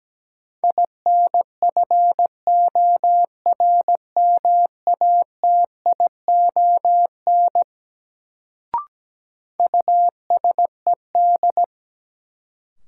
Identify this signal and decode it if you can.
Morse code, heard as a keyed tone, about 17 wpm, 700 Hz. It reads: INFORMATION USED
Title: Sentences from Top 600 Words - New Word Drill - 17wpm